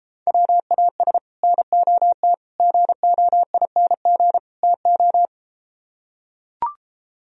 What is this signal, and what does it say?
Morse code, heard as a keyed tone, about 34 wpm, 700 Hz. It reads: WAS NOT GOING TO